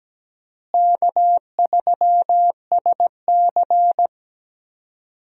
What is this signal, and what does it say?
Morse code, heard as a keyed tone, about 17 wpm, 700 Hz. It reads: K3SC
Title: USA Call Signs - 4 Characters - Repeated 2x - 17wpm